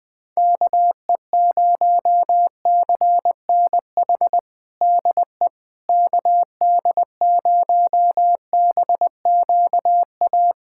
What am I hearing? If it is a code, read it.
Morse code, heard as a keyed tone, about 20 wpm, 700 Hz. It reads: KE0CNH DE KD0BQA
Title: USA Callsign DE Callsign - 6 Characters - 20wpm